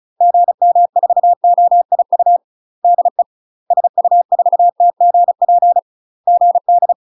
Morse code, heard as a keyed tone, about 36 wpm, 700 Hz. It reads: GM4OIU DE SU4TGP GD
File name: UK MASTER QSOs TRAINING / 35WPM / UK MASTER QSOs 0029